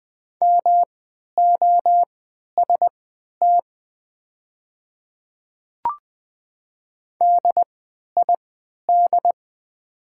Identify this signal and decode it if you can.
Morse code, heard as a keyed tone, about 20 wpm, spaced as for 11 wpm, 700 Hz. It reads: MOST DID
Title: 3x Character Spacing - Top 200 Words - 20wpm